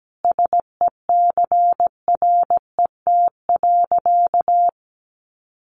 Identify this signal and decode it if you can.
Morse code, heard as a keyed tone, about 17 wpm, 700 Hz. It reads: SECRET.
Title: Sentences from Top 1200 Words 17wpm